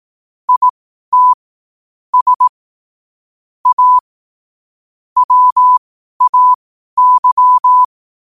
Morse code, heard as a keyed tone, about 18 wpm, spaced as for 12 wpm, 1000 Hz. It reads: IT S A WAY
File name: qotdfiles / 12wpm